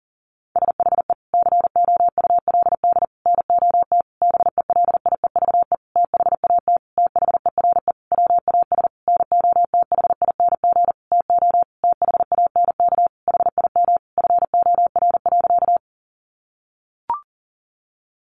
Morse code, heard as a keyed tone, about 40 wpm, 700 Hz. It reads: SHE COULD NOT BELIEVE THAT THERE WAS NOTHING TO THANK HIM FOR.
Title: Sentences from Top 1000 Words 40wpm